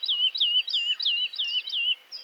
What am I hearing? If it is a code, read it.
There is no code.